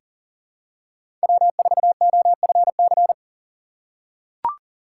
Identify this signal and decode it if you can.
Morse code, heard as a keyed tone, about 40 wpm, 700 Hz. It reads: W4OFC
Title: USA Call Signs - 6x Character Spacing - 40wpm